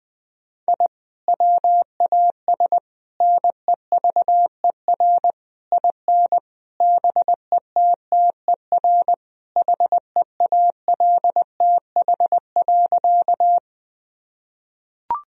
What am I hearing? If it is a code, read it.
Morse code, heard as a keyed tone, about 21 wpm, 700 Hz. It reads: I WAS NEVER IN BETTER HEALTH.
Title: Sentences from Top 1000 Words - Review Part 3 - 20wpm